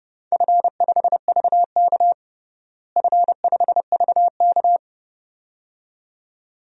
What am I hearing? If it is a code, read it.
Morse code, heard as a keyed tone, about 30 wpm, 700 Hz. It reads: F5VK F5VK
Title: DXCC-100 Call Signs - 4 Characters - Repeated 2x - 30wpm